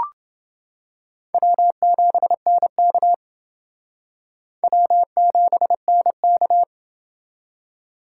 Morse code, heard as a keyed tone, about 28 wpm, 700 Hz. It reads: W7NK W7NK